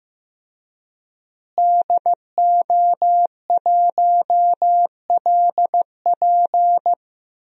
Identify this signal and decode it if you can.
Morse code, heard as a keyed tone, about 15 wpm, 700 Hz. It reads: DO1LP
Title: DXCC-100 Call Signs - 5 Characters - Repeated 2x - 15wpm